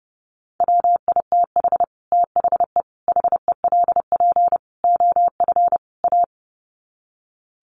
Morse code, heard as a keyed tone, about 30 wpm, 700 Hz. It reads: WITH THE HELP OF A
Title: Sets of 5 Words 30wpm